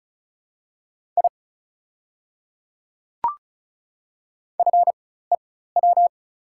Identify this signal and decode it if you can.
Morse code, heard as a keyed tone, about 36 wpm, spaced as for 16 wpm, 700 Hz. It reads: I FEW
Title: 4x Character Spacing - Top 300 Words - 35wpm